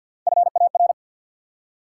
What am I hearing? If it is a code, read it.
Morse code, heard as a keyed tone, about 53 wpm, 700 Hz. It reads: FAR
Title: Three Letter Words - 50wpm